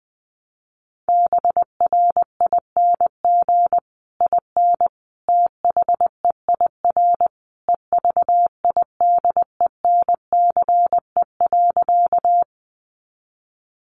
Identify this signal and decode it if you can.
Morse code, heard as a keyed tone, about 20 wpm, 700 Hz. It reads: BRING IN THEIR EVIDENCE.
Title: Sentences from Top 1300 Words 20wpm